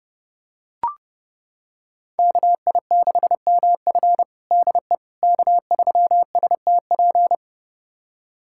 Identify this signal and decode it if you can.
Morse code, heard as a keyed tone, about 30 wpm, 700 Hz. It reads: KI6MF DE K3STP
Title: USA Callsign DE Callsign - 5 Characters - 30wpm